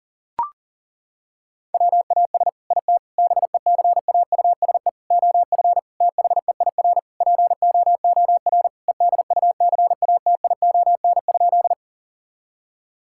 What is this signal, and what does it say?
Morse code, heard as a keyed tone, about 40 wpm, 700 Hz. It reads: WAS IT BECAUSE OF THEIR POOR EDUCATION?
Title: Sentences from Top 1000 Words - Review Part 6 - 40wpm